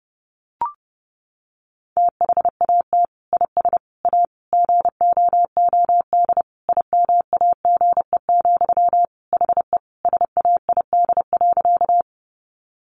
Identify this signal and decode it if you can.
Morse code, heard as a keyed tone, about 30 wpm, 700 Hz. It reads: THAT IS A GOOD IMAGE, HE SAID.
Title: Sentences from Top 700 Words 30wpm